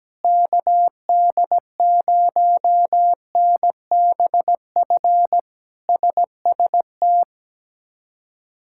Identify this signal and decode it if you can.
Morse code, heard as a keyed tone, about 17 wpm, 700 Hz. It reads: KD0NBF SST
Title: K1USN SST QSO Practice - 17wpm